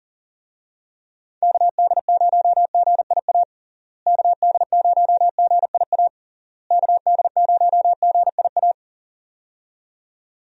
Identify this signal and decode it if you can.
Morse code, heard as a keyed tone, about 40 wpm, 700 Hz. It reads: KD0GIA KD0GIA KD0GIA
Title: USA Call Signs - 6 Characters - Repeated 3x - 40wpm